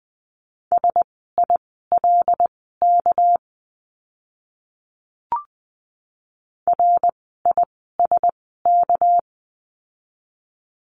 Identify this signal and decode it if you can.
Morse code, heard as a keyed tone, about 20 wpm, spaced as for 15 wpm, 700 Hz. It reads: SILK RISK
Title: 199 Lesson 32f - Letter k - Words 20wpm